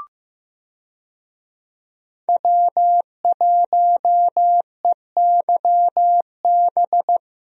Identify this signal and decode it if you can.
Morse code, heard as a keyed tone, about 15 wpm, 700 Hz. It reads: W1EYB
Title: USA Call Signs - 5 Characters - 15wpm